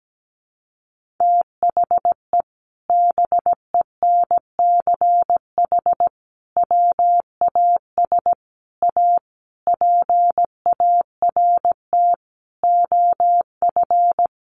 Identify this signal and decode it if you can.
Morse code, heard as a keyed tone, about 17 wpm, 700 Hz. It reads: THE BENCH WAS A PART OF
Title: Sentences from Top 1400 Words 17wpm